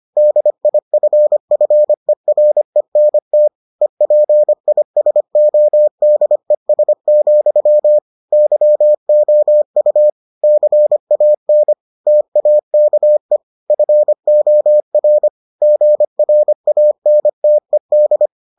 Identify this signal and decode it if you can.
Morse code, heard as a keyed tone, about 25 wpm, 600 Hz. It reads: DIFFERENT EPISODES, YOU CAN TAKE FOR GRANTED